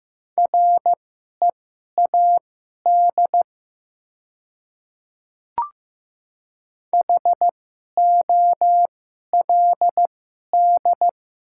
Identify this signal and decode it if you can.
Morse code, heard as a keyed tone, about 15 wpm, spaced as for 11 wpm, 700 Hz. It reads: READ HOLD
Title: 2x Character Spacing - Top 500 Words - 15wpm